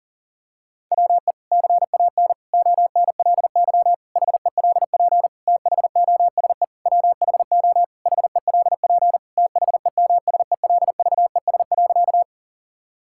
Morse code, heard as a keyed tone, about 40 wpm, 700 Hz. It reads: WE CAN ONLY HELP THOSE WHO HELP THEMSELVES.
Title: Sentences from Top 1000 Words - Review Part 5 - 40wpm